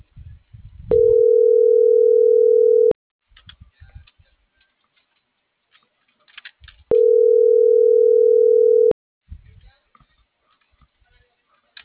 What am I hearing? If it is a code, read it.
Ringback tone (North America), 440 + 480 Hz.